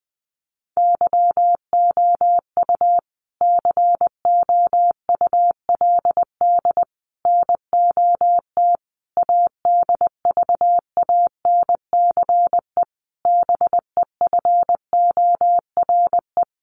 Morse code, heard as a keyed tone, about 20 wpm, 700 Hz. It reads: YOU COULD NOT ADVANCE BEFORE
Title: Sentences from Top 1400 Words 20wpm